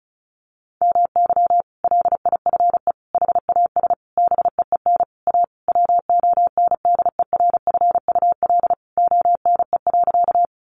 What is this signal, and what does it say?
Morse code, heard as a keyed tone, about 34 wpm, 700 Hz. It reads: MY LIFE HAS BEEN A WONDERFUL ONE.
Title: Sentences from Top 1100 Words 35wpm